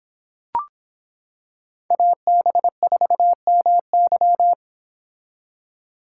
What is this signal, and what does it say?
Morse code, heard as a keyed tone, about 25 wpm, 700 Hz. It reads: AB4MY